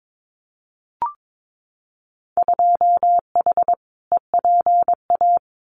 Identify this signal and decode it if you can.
Morse code, heard as a keyed tone, about 22 wpm, 700 Hz. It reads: 2H EPA
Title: Winter Field Day Exchanges Encore 22wpm